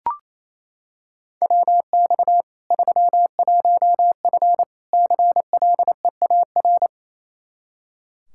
Morse code, heard as a keyed tone, about 28 wpm, 700 Hz. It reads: WX 31F CLEAR
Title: Practice QSO - Exchange 2 - Info - 28wpm